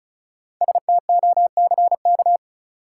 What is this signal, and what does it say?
Morse code, heard as a keyed tone, about 36 wpm, 700 Hz. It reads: STOCK
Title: Sets of 5 Words - New Word Drill - 35wpm